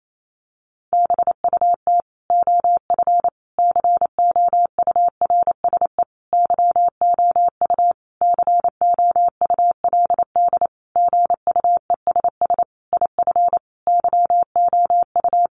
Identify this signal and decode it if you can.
Morse code, heard as a keyed tone, about 28 wpm, 700 Hz. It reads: BUT OF COURSE YOU COULD GUESS IF YOU
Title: Sentences from Top 1500 Words 28wpm